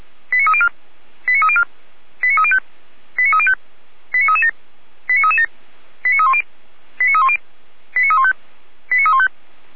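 Five-tone selective call (ZVEI), heard as quick runs of five tones, tones of about 70 ms. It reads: 89304 89304 89305 89305 89337 89337 89310 89310 89315 89315
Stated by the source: ZVEI five-tone selective call 89262 received on an SDR, logged by the Feuerwehr Orlamünde fire brigade.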